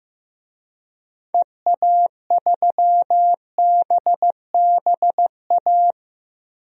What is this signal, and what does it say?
Morse code, heard as a keyed tone, about 15 wpm, 700 Hz. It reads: EA3BBA
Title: DXCC-100 Call Signs - 6 Characters - 15wpm